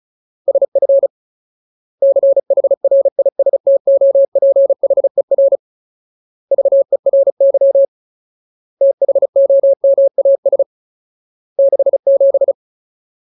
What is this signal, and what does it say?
Morse code, heard as a keyed tone, about 34 wpm, 550 Hz. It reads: SF CHRISTOPHER VERY THOMAS 67
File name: Ultimate Word List / 35WPM Ultimate